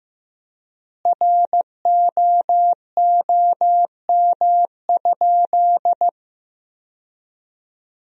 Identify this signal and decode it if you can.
Morse code, heard as a keyed tone, about 16 wpm, 700 Hz. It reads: ROOM?
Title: 4x Word-Spacing - Sentences from Top 300 Words - 15wpm